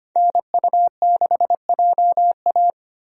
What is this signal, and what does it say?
Morse code, heard as a keyed tone, about 25 wpm, 700 Hz. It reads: NU6JA
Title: USA Call Signs - 5 Characters - Repeated 2x - 25wpm